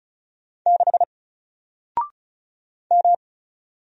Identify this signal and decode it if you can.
Morse code, heard as a keyed tone, about 36 wpm, 700 Hz. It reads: 6 M
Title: Single Letter-Number Alt 35wpm